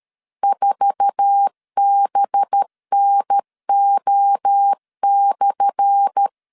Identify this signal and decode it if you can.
Morse code, heard as a keyed tone, about 13 wpm, 800 Hz. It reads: 4BNO/
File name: LEZIONI / 10 SIGNS / 10A SLASH 12wpm 2314